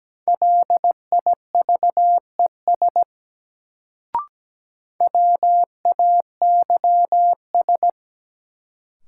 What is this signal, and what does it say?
Morse code, heard as a keyed tone, about 17 wpm, 700 Hz. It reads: LIVES WAYS
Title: Sets of 5 Words Encore - New Word Drill - 17wpm